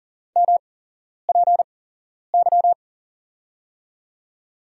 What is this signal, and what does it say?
Morse code, heard as a keyed tone, about 38 wpm, spaced as for 11 wpm, 700 Hz. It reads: MPY
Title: USA Call Signs - 6 Characters - 8x Character Spacing - 40wpm